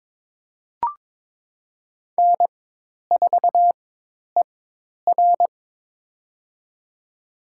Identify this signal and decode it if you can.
Morse code, heard as a keyed tone, about 22 wpm, spaced as for 10 wpm, 700 Hz. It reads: N4ER
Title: USA Call Signs - 4 Characters - 4x Character Spacing - 22wpm